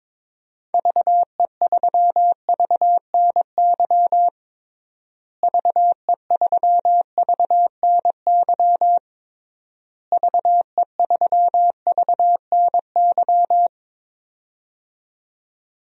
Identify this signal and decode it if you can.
Morse code, heard as a keyed tone, about 21 wpm, 700 Hz. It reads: VE3VNY VE3VNY VE3VNY
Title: DXCC-100 Call Signs - 6 Characters - Repeated 3x - 22wpm